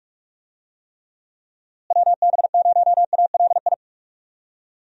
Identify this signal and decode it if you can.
Morse code, heard as a keyed tone, about 46 wpm, 700 Hz. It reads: WB0ALI
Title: USA Call Signs - 6 Characters - Repeated 2x - 45wpm